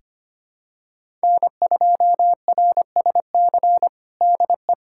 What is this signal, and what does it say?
Morse code, heard as a keyed tone, about 25 wpm, 700 Hz. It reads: N2RSC DE
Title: USA Callsign DE Callsign - 5 Characters - 25wpm